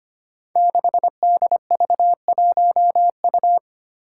Morse code, heard as a keyed tone, about 25 wpm, 700 Hz. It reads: 6DV1U
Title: Five Challenging Letter-Number Combinations 25wpm